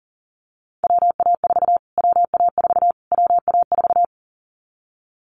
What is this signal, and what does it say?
Morse code, heard as a keyed tone, about 40 wpm, 700 Hz. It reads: WA4 WA4 WA4
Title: Callsign Prefixes for Top 25 DXCC - Repeated 3x - 40wpm